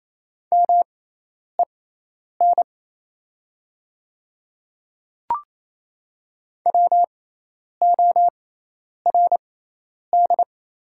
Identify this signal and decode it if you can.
Morse code, heard as a keyed tone, about 28 wpm, spaced as for 10 wpm, 700 Hz. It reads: MEN WORD
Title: 6x Character Spacing - Top 200 Words - 28wpm